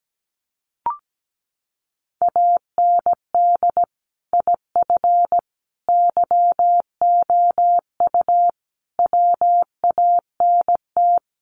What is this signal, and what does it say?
Morse code, heard as a keyed tone, about 17 wpm, 700 Hz. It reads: AND IF YOU WANT
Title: Sets of 5 Words Encore 17wpm